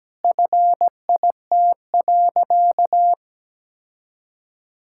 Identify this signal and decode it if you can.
Morse code, heard as a keyed tone, about 17 wpm, 700 Hz. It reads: FIT.
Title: Sentences from Top 1000 Words - Review Part 5 - 17wpm